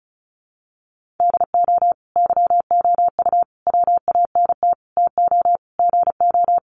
Morse code, heard as a keyed tone, about 34 wpm, 700 Hz. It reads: DO YOU WANT TO GO
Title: Sentences from Top 1400 Words 35wpm